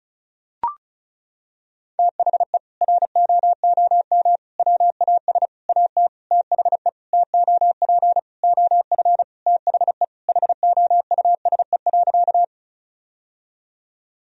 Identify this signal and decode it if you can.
Morse code, heard as a keyed tone, about 36 wpm, 700 Hz. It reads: THE ROOM WAS AT THE TOP OF THE HOUSE.
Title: Sentences from Top 1000 Words - Review Part 2 - 35wpm